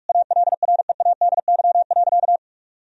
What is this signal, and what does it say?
Morse code, heard as a keyed tone, about 46 wpm, 700 Hz. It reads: ALREADY.
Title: Sentences from Top 1000 Words - Review Part 3 - 45wpm